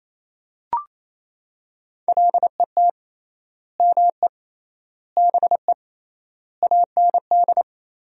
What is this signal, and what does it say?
Morse code, heard as a keyed tone, about 28 wpm, 700 Hz. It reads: LET ME BE AND
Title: 3x Word-Spacing - Sentences from Top 300 Words - 28wpm